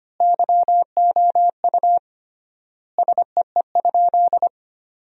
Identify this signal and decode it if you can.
Morse code, heard as a keyed tone, about 25 wpm, 700 Hz. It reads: YOU SEE?
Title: 3x Word-Spacing - Sentences from Top 100 Words - 25wpm